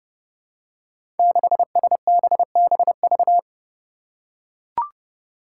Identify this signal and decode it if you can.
Morse code, heard as a keyed tone, about 30 wpm, 700 Hz. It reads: 6SBBV
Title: Five Challenging Letter-Number Combinations 30wpm